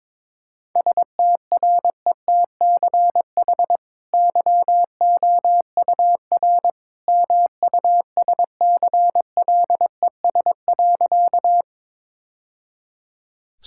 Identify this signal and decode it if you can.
Morse code, heard as a keyed tone, about 22 wpm, 700 Hz. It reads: STRETCH YOUR MUSCLES.
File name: practice / 22wpm / Easy Sentences with Hints - 22wpm